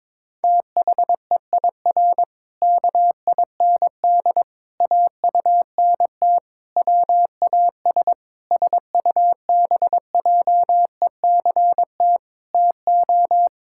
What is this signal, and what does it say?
Morse code, heard as a keyed tone, about 22 wpm, 700 Hz. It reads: THEIR KIND AUNT WAS SUBJECT TO
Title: Pangram Challenge v2 - 22wpm